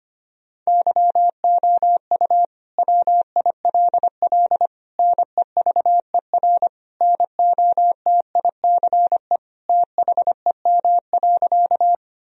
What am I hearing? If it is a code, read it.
Morse code, heard as a keyed tone, about 25 wpm, 700 Hz. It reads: YOU WILL NEVER NOTICE THEM.
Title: Sentences from Top 1000 Words - Review Part 2 - 25wpm